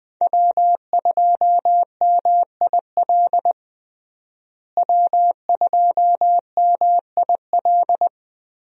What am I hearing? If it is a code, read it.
Morse code, heard as a keyed tone, about 20 wpm, 700 Hz. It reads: W2MIL W2MIL
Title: USA Call Signs - 5 Characters - Repeated 2x - 20wpm